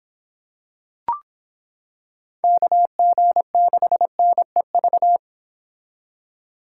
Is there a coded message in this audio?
Yes: Morse code from 2.4 to 5.2 s, heard as a keyed tone, about 25 wpm, 700 Hz. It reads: KG6NEV